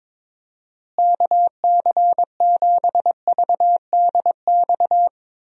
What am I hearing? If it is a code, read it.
Morse code, heard as a keyed tone, about 22 wpm, 700 Hz. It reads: KC7VDX
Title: USA Call Signs - 6 Characters - Repeated 3x - 22wpm